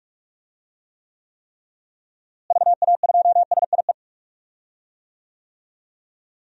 Morse code, heard as a keyed tone, about 42 wpm, 700 Hz. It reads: VA2SIE
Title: DXCC-100 Call Signs - 6 Characters - Repeated 3x - 45wpm